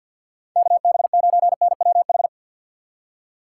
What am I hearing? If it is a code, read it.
Morse code, heard as a keyed tone, about 65 wpm, spaced as for 57 wpm, 700 Hz. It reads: KB9NWH
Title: Speed Racing - USA 2x3 Call Signs 50wpm